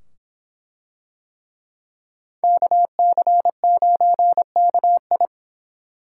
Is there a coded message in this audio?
Yes: Morse code from 2.4 to 5.3 s, heard as a keyed tone, about 25 wpm, 700 Hz. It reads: KC9KI